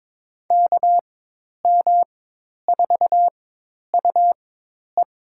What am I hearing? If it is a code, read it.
Morse code, heard as a keyed tone, about 22 wpm, spaced as for 10 wpm, 700 Hz. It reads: KM4UE
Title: USA Call Signs - 4x Character Spacing - 22wpm